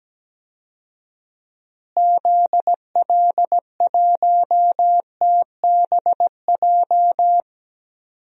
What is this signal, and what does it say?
Morse code, heard as a keyed tone, about 17 wpm, 700 Hz. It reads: ZL1TBJ
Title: DXCC-100 Call Signs - 6 Characters - Repeated 2x - 17wpm